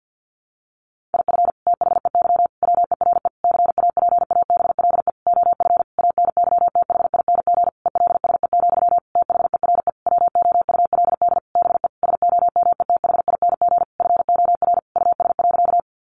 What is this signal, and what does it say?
Morse code, heard as a keyed tone, about 48 wpm, 700 Hz. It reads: IF THEY WERE CAPABLE OF ANYTHING ELSE, THERE WOULD BE SOMETHING FOR US.
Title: Sentences from Top 1200 Words 50wpm